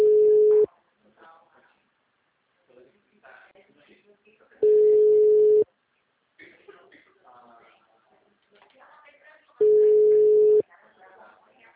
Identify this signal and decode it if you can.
Ringback tone (Europe, 425 Hz), with speech.